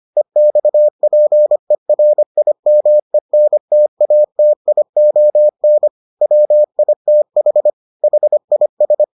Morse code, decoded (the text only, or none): EXPERIMENTATION WITH HIS